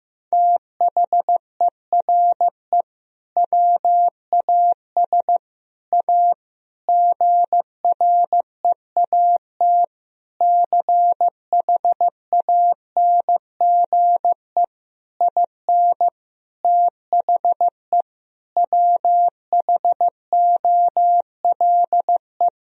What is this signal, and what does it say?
Morse code, heard as a keyed tone, about 15 wpm, 700 Hz. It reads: THERE WAS A GREAT CHANGE IN THE WHOLE
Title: Sentences from Top 1000 Words - Review Part 3 - 15wpm